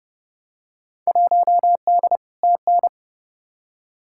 Morse code, heard as a keyed tone, about 30 wpm, 700 Hz. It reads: 1D TN
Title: Field Day Exchanges - 30wpm